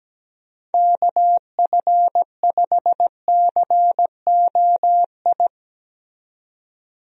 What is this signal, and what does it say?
Morse code, heard as a keyed tone, about 17 wpm, 700 Hz. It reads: KF5COI
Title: USA Call Signs - 6 Characters - 17wpm